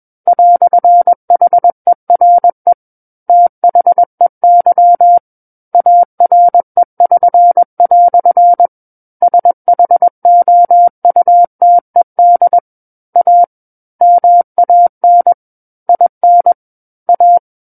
Morse code, heard as a keyed tone, about 21 wpm, 700 Hz. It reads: "HERE THEY ARE<SN>" SHOUTED A MAN IN A
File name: Book One / 12-Chapter 12 - 21 WPM